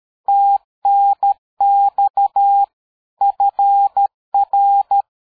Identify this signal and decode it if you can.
Morse code, heard as a keyed tone, about 13 wpm, 800 Hz. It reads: TNX FR
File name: dk5ke / QSO 60BpM